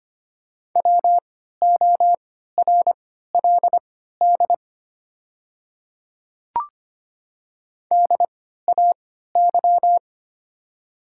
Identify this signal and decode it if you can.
Morse code, heard as a keyed tone, about 25 wpm, spaced as for 14 wpm, 700 Hz. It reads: WORLD DAY
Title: 3x Character Spacing - Top 200 Words - 25wpm